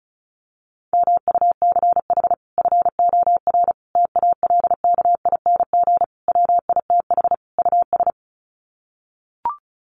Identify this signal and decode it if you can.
Morse code, heard as a keyed tone, about 34 wpm, 700 Hz. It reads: MUCH FOR TALKING WITH US
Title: Sets of 5 Words Encore 35wpm